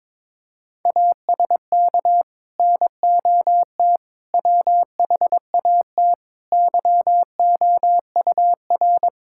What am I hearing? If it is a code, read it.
Morse code, heard as a keyed tone, about 22 wpm, 700 Hz. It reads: ASK NOT WHAT YOUR
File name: practice / 22wpm / Sets of 8 Words - 22wpm